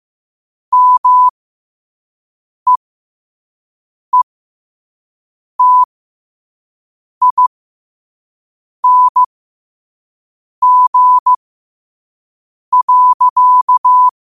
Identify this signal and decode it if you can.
Morse code, heard as a keyed tone, about 15 wpm, spaced as for 5 wpm, 1000 Hz. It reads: MEETING.